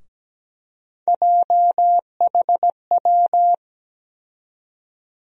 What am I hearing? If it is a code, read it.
Morse code, heard as a keyed tone, about 17 wpm, 700 Hz. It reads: JHW
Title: USA Call Sign Postfixes - 3 Letters 17wpm